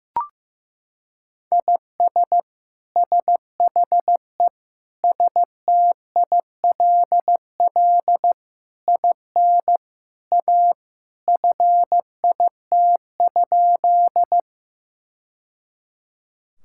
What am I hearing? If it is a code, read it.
Morse code, heard as a keyed tone, about 14 wpm, 700 Hz. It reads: IS SHE STILL IN A FIT?
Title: Sentences from Top 1000 Words - Review Part 5 - 15wpm